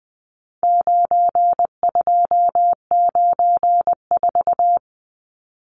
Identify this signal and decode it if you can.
Morse code, heard as a keyed tone, about 20 wpm, 700 Hz. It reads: 9294
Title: Numbers - 4-Digit 20wpm